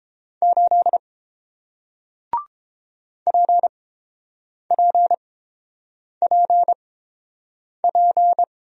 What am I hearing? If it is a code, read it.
Morse code, heard as a keyed tone, about 30 wpm, spaced as for 7 wpm, 700 Hz. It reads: 8 PPPP